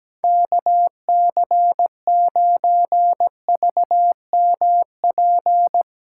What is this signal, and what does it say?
Morse code, heard as a keyed tone, about 19 wpm, 700 Hz. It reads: KC9VMP
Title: Speed Racing - USA 2x3 Call Signs 17wpm